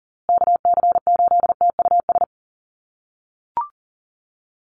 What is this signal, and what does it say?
Morse code, heard as a keyed tone, about 40 wpm, 700 Hz. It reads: KC8TUS
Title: USA Call Signs Repeated 2x 40wpm